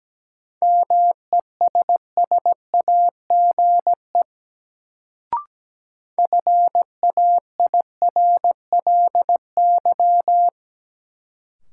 Morse code, heard as a keyed tone, about 17 wpm, 700 Hz. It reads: MESSAGE FAIRLY
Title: Sentences from Top 1100 Words - New Word Drill - 17wpm